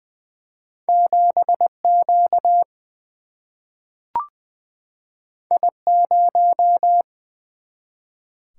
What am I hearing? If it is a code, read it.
Morse code, heard as a keyed tone, about 20 wpm, 700 Hz. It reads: 7 Q I 0
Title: Single-Letter Single-Number 20wpm